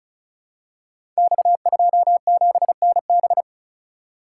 Morse code, heard as a keyed tone, about 36 wpm, 700 Hz. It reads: X27NB
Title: Code Groups - 5 Letter:Numbers - 35wpm